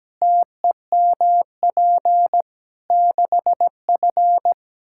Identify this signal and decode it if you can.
Morse code, heard as a keyed tone, about 17 wpm, 700 Hz. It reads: TEMP 6F
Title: Practice QSO - Exchange 2 - Info - 17wpm